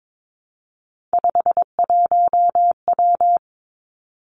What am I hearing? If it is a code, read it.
Morse code, heard as a keyed tone, about 22 wpm, 700 Hz. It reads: 51W
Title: Three Challenging Letter-Number Combinations 22wpm